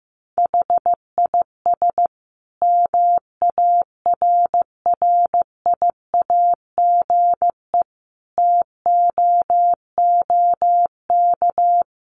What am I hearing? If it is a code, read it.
Morse code, heard as a keyed tone, about 15 wpm, 700 Hz. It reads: HIS MARRIAGE TOOK
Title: Sentences from Top 1500 Words 15wpm